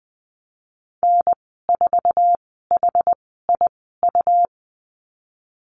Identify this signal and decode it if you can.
Morse code, heard as a keyed tone, about 20 wpm, spaced as for 15 wpm, 700 Hz. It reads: N4HIU
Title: 147 Lesson 24g - Letter y - Call Signs 20wpm